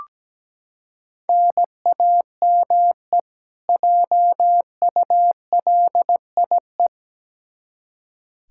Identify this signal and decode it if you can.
Morse code, heard as a keyed tone, about 17 wpm, 700 Hz. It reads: NAME JULIE
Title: K1USN SST QSO Practice 17wpm